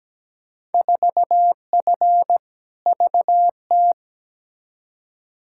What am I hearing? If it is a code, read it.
Morse code, heard as a keyed tone, about 17 wpm, 700 Hz. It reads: 4F VT
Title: Field Day Exchanges Encore - 17wpm